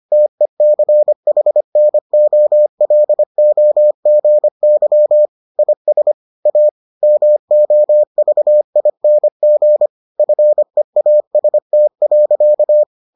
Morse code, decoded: TECHNOLOGY IS A MOVING FEAST.

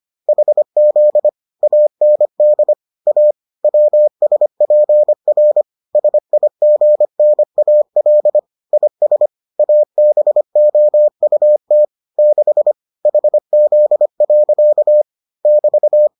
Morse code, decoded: HZ AND A WSPR SIGNAL IS ABOUT 6 HZ. <BT>

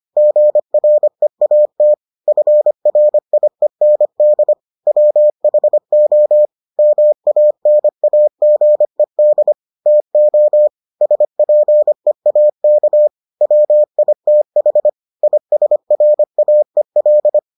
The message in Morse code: GREAT FRIEND WHO MANAGED TO SPEAK WITH ISRAEL

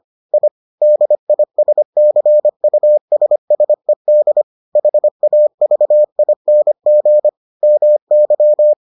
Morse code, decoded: I DISCUSSED HAVING MY

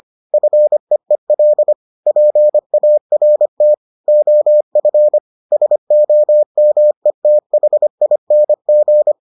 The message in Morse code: FEEL PART OF SOMETHING

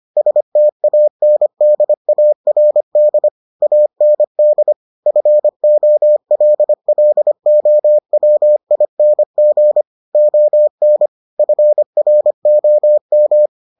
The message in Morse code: STANDARD AND FOLLOWING ON FROM

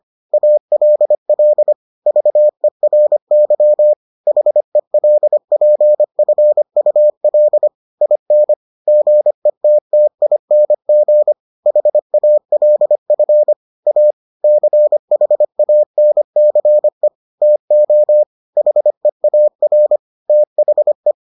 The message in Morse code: ALL VERY HELPFUL IN GETTING HALF A CHANCE TO HEAR THE

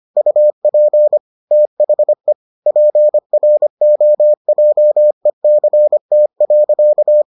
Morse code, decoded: UP THE PROJECT.

